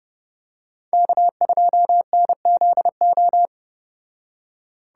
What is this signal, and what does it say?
Morse code, heard as a keyed tone, about 29 wpm, 700 Hz. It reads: K2NZO